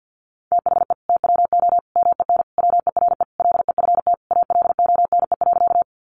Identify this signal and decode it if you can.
Morse code, heard as a keyed tone, about 50 wpm, 700 Hz. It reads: THE TWO MEN WERE LEFT ALONE.
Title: Sentences from Top 1000 Words 50wpm